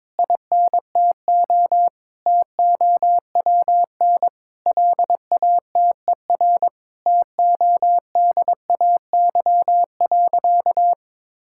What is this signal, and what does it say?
Morse code, heard as a keyed tone, about 22 wpm, 700 Hz. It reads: INTO TOWN LATER TODAY.
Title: Sentences from Top 1000 Words - Review Part 6 - 22wpm